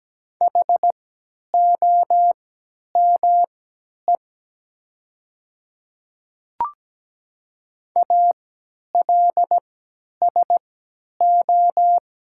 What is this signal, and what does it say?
Morse code, heard as a keyed tone, about 17 wpm, spaced as for 10 wpm, 700 Hz. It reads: HOME ALSO
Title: 3x Character Spacing - Top 200 Words - 17wpm